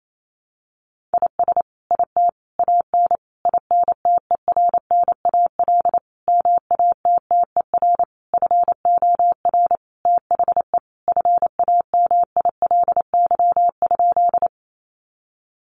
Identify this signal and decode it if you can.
Morse code, heard as a keyed tone, about 28 wpm, 700 Hz. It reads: IS IT AN INTERNAL MATTER FOR THE FAMILY?
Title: Sentences from Top 1400 Words 28wpm